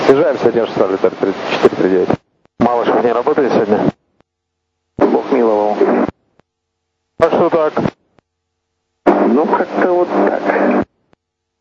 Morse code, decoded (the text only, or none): none